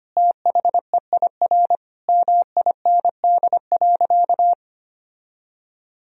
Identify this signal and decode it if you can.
Morse code, heard as a keyed tone, about 25 wpm, 700 Hz. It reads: THEIR MIND.
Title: Sentences from Top 1000 Words - Review Part 3 - 25wpm